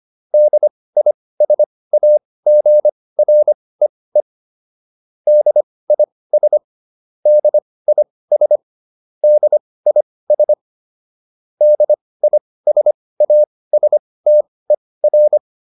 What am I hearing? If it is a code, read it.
Morse code, heard as a keyed tone, about 25 wpm, spaced as for 18 wpm, 600 Hz. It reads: DISAGREE DIS DIS DIS DISASTER
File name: DIS8 18